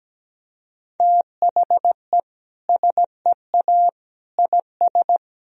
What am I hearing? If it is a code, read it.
Morse code, heard as a keyed tone, about 17 wpm, 700 Hz. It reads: THE SEA IS
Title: Sentences from Top 1000 Words - Review Part 1 - 17wpm